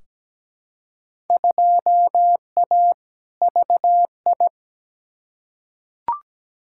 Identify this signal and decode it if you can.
Morse code, heard as a keyed tone, about 17 wpm, 700 Hz. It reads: 2A VI
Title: Field Day Exchanges - 17wpm